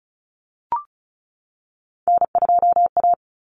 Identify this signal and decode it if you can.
Morse code, heard as a keyed tone, about 34 wpm, 700 Hz. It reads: N2A
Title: Common Cut Numbers - 3-Digit 35wpm